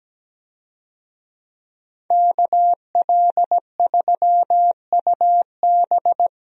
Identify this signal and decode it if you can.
Morse code, heard as a keyed tone, about 17 wpm, 700 Hz. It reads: KL3UB
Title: USA Call Signs - 5 Characters - Repeated 2x - 17wpm